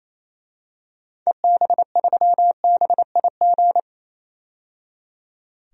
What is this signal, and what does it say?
Morse code, heard as a keyed tone, about 28 wpm, 700 Hz. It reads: EB3BIG
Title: DXCC-100 Call Signs - 6 Characters - 28wpm